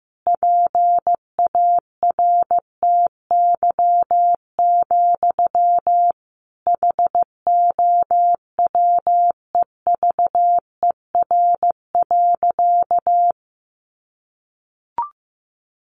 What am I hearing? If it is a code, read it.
Morse code, heard as a keyed tone, about 15 wpm, 700 Hz. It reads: PARTY, HOWEVER.
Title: Sentences from Top 1400 Words 15wpm